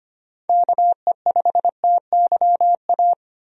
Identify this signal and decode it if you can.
Morse code, heard as a keyed tone, about 25 wpm, 700 Hz. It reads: KE5TYA
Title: USA Call Signs - 6 Characters - 25wpm